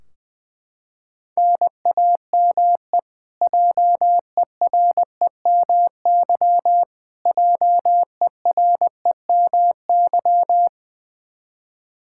Morse code, heard as a keyed tone, about 20 wpm, 700 Hz. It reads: NAME JEREMY JEREMY
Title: Practice QSO - Exchange 1 - Beginning - 20wpm